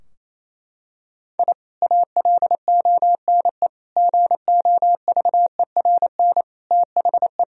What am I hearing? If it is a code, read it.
Morse code, heard as a keyed tone, about 28 wpm, 700 Hz. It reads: I ALONE GOVERN THE
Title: Sentences from Top 1000 Words - Review Part 2 - 28wpm